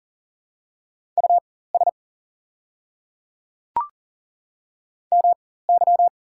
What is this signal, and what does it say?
Morse code, heard as a keyed tone, about 40 wpm, spaced as for 19 wpm, 690 Hz. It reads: US MY
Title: 4x Character Spacing - Top 200 Words - 40wpm